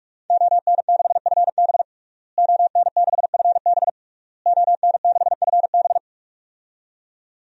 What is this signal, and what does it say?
Morse code, heard as a keyed tone, about 41 wpm, 700 Hz. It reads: ON6FB ON6FB ON6FB
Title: DXCC-100 Call Signs - 5 Characters - Repeated 3x - 45wpm